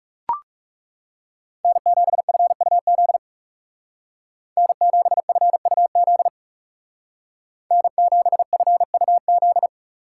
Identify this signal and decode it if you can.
Morse code, heard as a keyed tone, about 37 wpm, 700 Hz. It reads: N7FUZ N7FUZ N7FUZ
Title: Speed Racing - USA Call Signs 30wpm